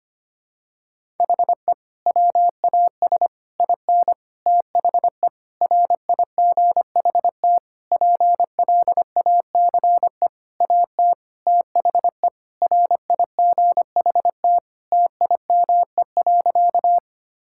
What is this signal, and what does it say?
Morse code, heard as a keyed tone, about 25 wpm, 700 Hz. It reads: HE WAS IN THE RIGHT PLACE AT THE RIGHT TIME.
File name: practice / 25wpm / Sentences with 9-Word Phrases - 25wpm